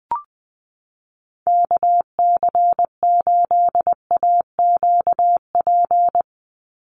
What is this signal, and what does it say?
Morse code, heard as a keyed tone, about 20 wpm, 700 Hz. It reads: KC8AQP
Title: USA Call Signs 20wpm